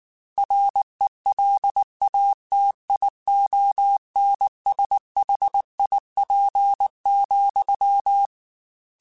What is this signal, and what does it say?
Morse code, heard as a keyed tone, about 19 wpm, 800 Hz. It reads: RELATIONSHIP,